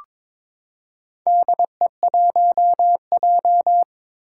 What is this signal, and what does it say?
Morse code, heard as a keyed tone, about 23 wpm, 700 Hz. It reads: DE1J
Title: Four Challenging Letter-Number Combinations 22wpm